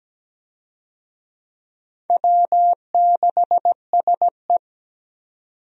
Morse code, heard as a keyed tone, about 17 wpm, 700 Hz. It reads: W6SE
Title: USA Call Signs - 4 Characters - 17wpm